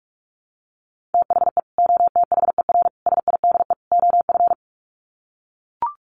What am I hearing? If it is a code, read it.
Morse code, heard as a keyed tone, about 44 wpm, 700 Hz. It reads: THE OTHER SIDE OF
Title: Sets of 4 Words 45wpm